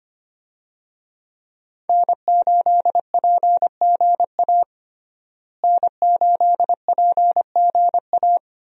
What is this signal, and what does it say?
Morse code, heard as a keyed tone, about 25 wpm, 700 Hz. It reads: N8PGA N8PGA
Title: USA Call Signs - 5 Characters - Repeated 2x - 25wpm